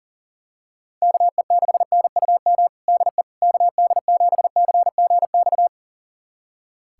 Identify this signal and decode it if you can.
Morse code, heard as a keyed tone, about 40 wpm, 700 Hz. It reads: KE6NUM DE KD7CGX
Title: USA Callsign DE Callsign - 6 Characters - 40wpm